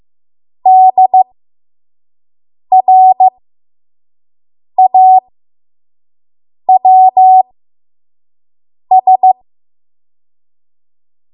Morse code, heard as a keyed tone, about 15 wpm, spaced as for 5 wpm, 750 Hz. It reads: DRAWS
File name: Morse / 05WPM